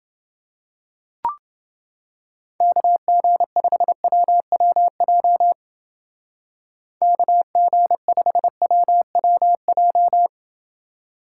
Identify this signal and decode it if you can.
Morse code, heard as a keyed tone, about 29 wpm, 700 Hz. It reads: KG5WWJ KG5WWJ